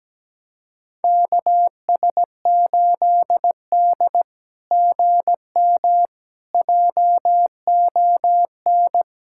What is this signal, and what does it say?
Morse code, heard as a keyed tone, about 17 wpm, 690 Hz. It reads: KS8D GM JON
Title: SOTA QSO Practice 17wpm